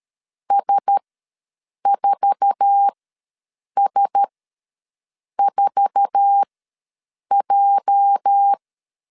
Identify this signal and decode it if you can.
Morse code, heard as a keyed tone, about 13 wpm, spaced as for 7 wpm, 800 Hz. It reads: S4S4J
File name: LEZIONI / 8 YJ45 / 8 YJ4 ws 12wpm 3249